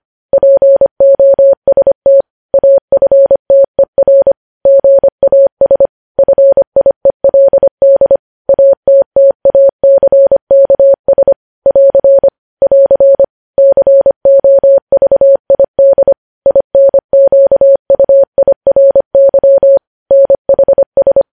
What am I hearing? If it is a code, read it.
Morse code, heard as a keyed tone, about 25 wpm, 550 Hz. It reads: POST AFTER GAS FIELD ATTACKS <AR> <AR> COVID INQUIRY NHS
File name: podcast / News headlines-1x-25x25